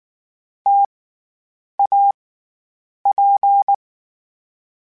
Morse code, heard as a keyed tone, about 19 wpm, 800 Hz. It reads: T A P
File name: Variable Spacing / Variable Spacing-mp3s / T AOEs @18